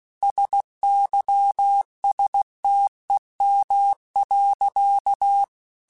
Morse code, heard as a keyed tone, about 16 wpm, 800 Hz. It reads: SYSTEM.
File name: Technical Subjects / Technical Subjects-mp3s / Sw Pwr Sup @15